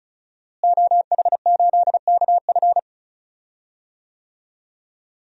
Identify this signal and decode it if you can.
Morse code, heard as a keyed tone, about 34 wpm, 700 Hz. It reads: OH8KF
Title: DXCC-100 Call Signs - 5 Characters - Repeated 3x - 35wpm